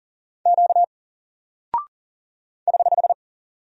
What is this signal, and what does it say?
Morse code, heard as a keyed tone, about 40 wpm, 700 Hz. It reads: Q <HH>
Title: Single Character (Letter:Number:Punctuation:Prosign) Alt 40wpm